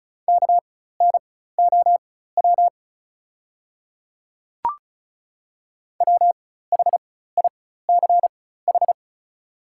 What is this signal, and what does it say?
Morse code, heard as a keyed tone, about 36 wpm, spaced as for 16 wpm, 690 Hz. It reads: KNOW WHICH
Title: 4x Character Spacing - Top 200 Words - 35wpm